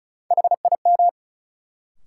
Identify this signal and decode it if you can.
Morse code, heard as a keyed tone, about 34 wpm, 700 Hz. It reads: HIM
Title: Sentences from Top 100 Words - New Word Drill - 35wpm